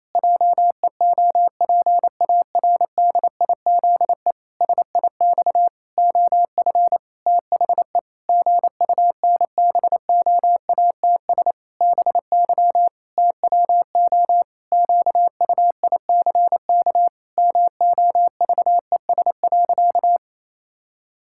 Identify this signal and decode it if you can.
Morse code, heard as a keyed tone, about 28 wpm, 700 Hz. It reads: JEOPARDIZE SIX OF THE GUNBOATS BY TWO QUICK MOVES.
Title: Pangram Challenge v2 - 28wpm